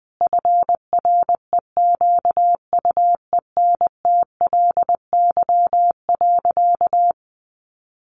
Morse code, heard as a keyed tone, about 20 wpm, 700 Hz. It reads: FREQUENTLY.
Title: Sentences from Top 1300 Words 20wpm